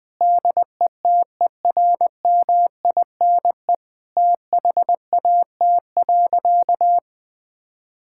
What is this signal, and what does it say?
Morse code, heard as a keyed tone, about 20 wpm, 700 Hz. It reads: DETERMINE THAT.
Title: Sentences from Top 1000 Words - Review Part 5 - 20wpm